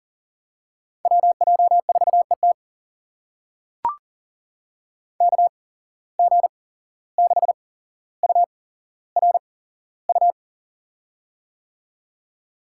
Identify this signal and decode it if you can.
Morse code, heard as a keyed tone, about 40 wpm, spaced as for 16 wpm, 700 Hz. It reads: WJ4ET KG6URU
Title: USA Call Signs - 8x Character Spacing - 40wpm